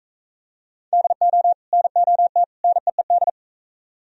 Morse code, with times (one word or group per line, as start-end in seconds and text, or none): 0.9-1.5 DO
1.7-2.4 NOT
2.6-3.3 NEED